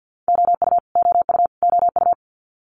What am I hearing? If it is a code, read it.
Morse code, heard as a keyed tone, about 50 wpm, 700 Hz. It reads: OU OU OU
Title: Multi-letter Phonemes Repeated 3x 50wpm